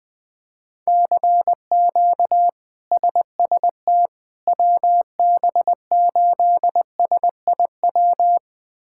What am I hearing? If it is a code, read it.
Morse code, heard as a keyed tone, about 20 wpm, 700 Hz. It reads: CQ SST WB8SIW
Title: K1USN SST QSO Practice - 20wpm